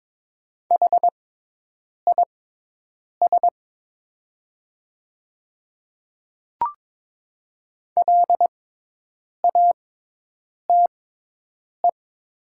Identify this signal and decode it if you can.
Morse code, heard as a keyed tone, about 22 wpm, spaced as for 8 wpm, 700 Hz. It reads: HIS LATE
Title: 6x Character Spacing - Top 300 Words - 22wpm